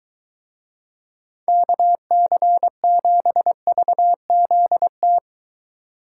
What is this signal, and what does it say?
Morse code, heard as a keyed tone, about 23 wpm, 700 Hz. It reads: KC7VZT